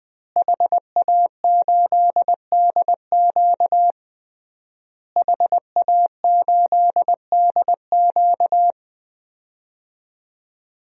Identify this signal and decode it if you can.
Morse code, heard as a keyed tone, about 20 wpm, 700 Hz. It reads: HA8DQ HA8DQ
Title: DXCC-100 Call Signs - 5 Characters - Repeated 3x - 20wpm